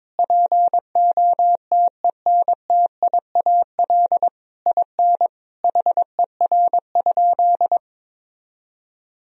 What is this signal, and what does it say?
Morse code, heard as a keyed tone, about 22 wpm, 700 Hz. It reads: POTENTIAL IN HER?
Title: Sentences from Top 1000 Words - Review Part 4 - 22wpm